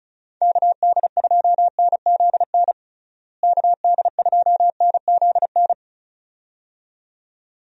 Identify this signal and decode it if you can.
Morse code, heard as a keyed tone, about 36 wpm, 700 Hz. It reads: KD2NZN KD2NZN
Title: USA Call Signs - 6 Characters - Repeated 2x - 35wpm